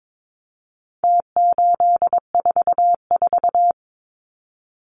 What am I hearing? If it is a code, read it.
Morse code, heard as a keyed tone, about 22 wpm, 700 Hz. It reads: T844
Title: Numbers - Contest Style - 4-Digit 22wpm